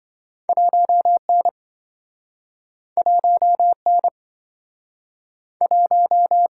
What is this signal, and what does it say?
Morse code, heard as a keyed tone, about 27 wpm, 700 Hz. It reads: 1N 1N 1